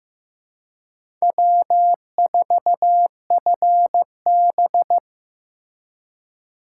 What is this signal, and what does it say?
Morse code, heard as a keyed tone, about 15 wpm, 700 Hz. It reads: W4FB
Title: USA Call Signs - 4x Character Spacing - 15wpm